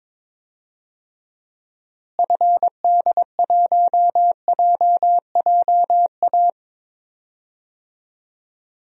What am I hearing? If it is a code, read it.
Morse code, heard as a keyed tone, about 22 wpm, 700 Hz. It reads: FD1JJA
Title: DXCC-100 Call Signs - 6 Characters - Repeated 3x - 22wpm